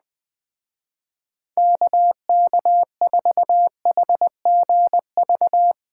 Morse code, heard as a keyed tone, about 20 wpm, 700 Hz. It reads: KK4HGV
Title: Speed Racing - USA 2x3 Call Signs 17wpm